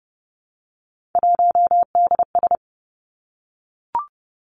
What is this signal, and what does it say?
Morse code, heard as a keyed tone, about 32 wpm, 700 Hz. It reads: 1DS
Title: Three Challenging Letter-Number Combinations 30wpm